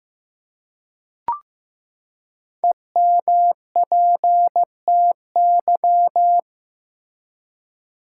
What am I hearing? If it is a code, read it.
Morse code, heard as a keyed tone, about 15 wpm, 700 Hz. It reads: EMPTY
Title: Five Letter Words - 15wpm